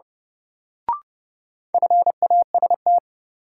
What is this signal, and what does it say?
Morse code, heard as a keyed tone, about 29 wpm, 700 Hz. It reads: FAST
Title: Sentences from Top 400 Words - New Word Drill - 30wpm